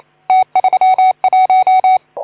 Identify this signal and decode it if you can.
Morse code, heard as a keyed tone, about 31 wpm, 750 Hz. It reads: T31
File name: Archiv / CUC181 / OK2ZC CUC 181